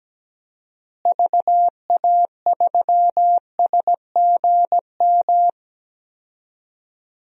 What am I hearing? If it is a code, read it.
Morse code, heard as a keyed tone, about 17 wpm, 700 Hz. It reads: VA3SGM
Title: DXCC-100 Call Signs - 6 Characters - Repeated 2x - 17wpm